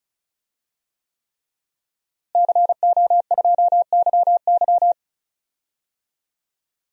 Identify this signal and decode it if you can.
Morse code, heard as a keyed tone, about 36 wpm, 700 Hz. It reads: CO2YY
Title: DXCC-100 Call Signs - 5 Characters - Repeated 3x - 35wpm